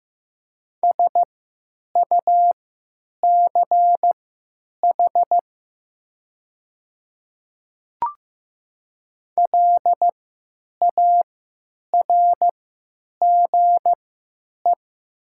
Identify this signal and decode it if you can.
Morse code, heard as a keyed tone, about 15 wpm, spaced as for 9 wpm, 700 Hz. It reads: SUCH LARGE
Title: 3x Character Spacing - Top 200 Words - 15wpm